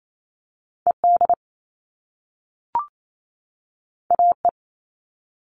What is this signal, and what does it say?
Morse code, heard as a keyed tone, about 28 wpm, 700 Hz. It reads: ED AE
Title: All Cut Numbers - 2-Digit 28wpm